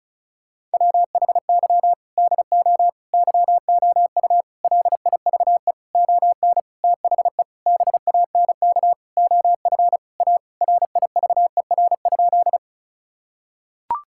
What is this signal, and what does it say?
Morse code, heard as a keyed tone, about 34 wpm, 700 Hz. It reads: WHY DO YOU LIVE ON THE BANK OF A RIVER?
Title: Sentences from Top 1000 Words - Review Part 6 - 35wpm